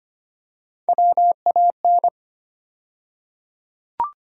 Morse code, heard as a keyed tone, about 25 wpm, 700 Hz. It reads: WAN
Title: USA Call Sign Postfixes - 3 Letters 25wpm